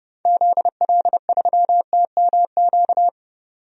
Morse code, heard as a keyed tone, about 30 wpm, 700 Hz. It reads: ZL3TMQ
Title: DXCC-100 Call Signs - 6 Characters - 30wpm